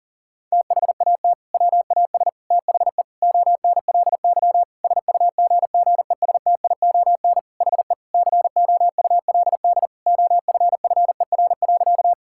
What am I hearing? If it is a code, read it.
Morse code, heard as a keyed tone, about 37 wpm, 700 Hz. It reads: THAT WAS THE ONLY SUGGESTION HE COULD OFFER.
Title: Sentences from Top 1600 Words 40wpm